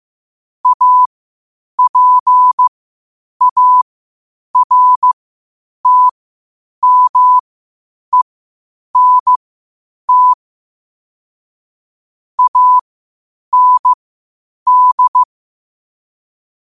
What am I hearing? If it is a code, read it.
Morse code, heard as a keyed tone, about 15 wpm, spaced as for 8 wpm, 1000 Hz. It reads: APARTMENT AND